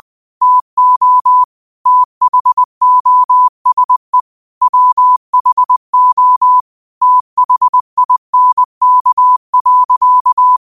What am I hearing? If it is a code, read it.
Morse code, heard as a keyed tone, about 20 wpm, 1000 Hz. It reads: TO THOSE WHO THINK.
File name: qotdfiles / 20wpm